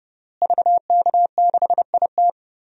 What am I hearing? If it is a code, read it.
Morse code, heard as a keyed tone, about 32 wpm, 700 Hz. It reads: VK6IT
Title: DXCC-100 Call Signs - 5 Characters - Repeated 3x - 30wpm